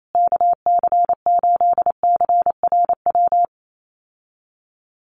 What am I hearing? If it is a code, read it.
Morse code, heard as a keyed tone, about 28 wpm, 700 Hz. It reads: KC8CRW
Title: USA Call Signs Repeated 2x 28wpm